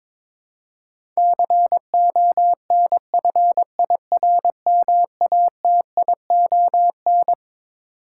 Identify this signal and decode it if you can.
Morse code, heard as a keyed tone, about 21 wpm, 700 Hz. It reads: CONFIRMATION
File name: Twelve Letter Words - 22wpm